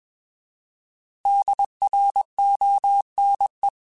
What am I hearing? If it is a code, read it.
Morse code, heard as a keyed tone, about 21 wpm, 800 Hz. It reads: DRONE
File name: Alphabetical / Alphabetical mp3s / Ds @20